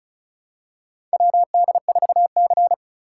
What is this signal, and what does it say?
Morse code, heard as a keyed tone, about 34 wpm, 700 Hz. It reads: WD4C
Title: USA Call Signs - 4 Characters - 3x Character Spacing - 35wpm